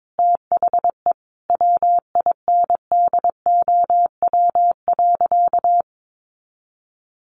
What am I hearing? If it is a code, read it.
Morse code, heard as a keyed tone, about 22 wpm, 700 Hz. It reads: THE WINDOW.
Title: Additional Easy Phrases 22wpm